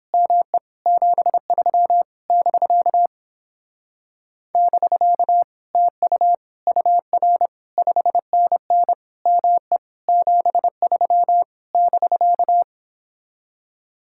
Morse code, heard as a keyed tone, about 29 wpm, 700 Hz. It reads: ME 73 <BK> <BK> TU UR 5NN ME 73 <BK>